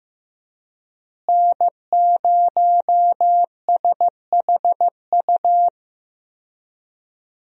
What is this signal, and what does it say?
Morse code, heard as a keyed tone, about 15 wpm, 700 Hz. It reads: N0SHU
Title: USA Call Signs - 5 Characters - Repeated 2x - 15wpm